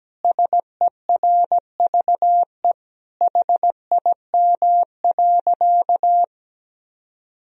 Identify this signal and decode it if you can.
Morse code, heard as a keyed tone, about 17 wpm, 700 Hz. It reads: SERVE HIM.
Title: Sentences from Top 1000 Words - Review Part 2 - 17wpm